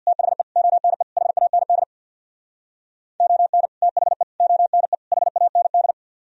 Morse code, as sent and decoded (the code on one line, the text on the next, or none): - .... / --- -. / .... .- -. -.. / --- -. / - .... / --- -. / .... .- -. -..
THE ONE HAND ON THE ONE HAND